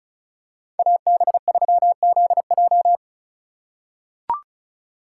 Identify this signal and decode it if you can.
Morse code, heard as a keyed tone, about 36 wpm, 700 Hz. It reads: AB3ZJ
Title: USA Call Signs - 5 Characters - 4x Character Spacing - 35wpm